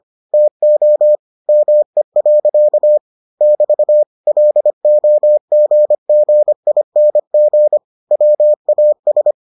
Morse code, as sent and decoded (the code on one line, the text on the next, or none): - --- / -- . .-.-.- / -...- / .-.. --- --. --. .. -. --. / .-- .- ...
TO ME. <BT> LOGGING WAS